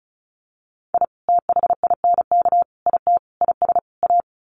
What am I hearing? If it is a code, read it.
Morse code, heard as a keyed tone, about 34 wpm, 700 Hz. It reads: I THINK IT IS A
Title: Sets of 5 Words Encore 35wpm